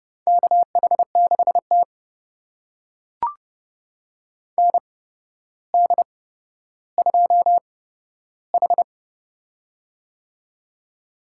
Morse code, heard as a keyed tone, about 29 wpm, spaced as for 13 wpm, 700 Hz. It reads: KH6T ND2H
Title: USA Call Signs - 4 Characters - 8x Character Spacing - 30wpm